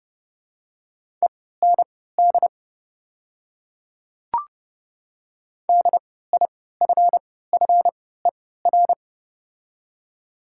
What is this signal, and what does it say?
Morse code, heard as a keyed tone, about 30 wpm, spaced as for 17 wpm, 700 Hz. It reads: END DIFFER
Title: 3x Character Spacing - Top 300 Words - 30wpm